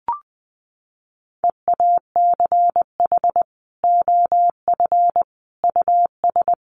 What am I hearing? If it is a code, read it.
Morse code, heard as a keyed tone, about 20 wpm, 700 Hz. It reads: EACH OF US
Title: Sentences from Top 500 Words 20wpm